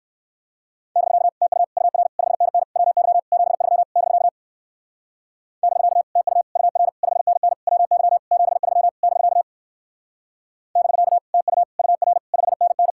Morse code, heard as a keyed tone, about 62 wpm, 700 Hz. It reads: <BK> TU UR 5NN WY 73 <BK> <BK> TU UR 5NN WY 73 <BK> <BK> TU UR 5NN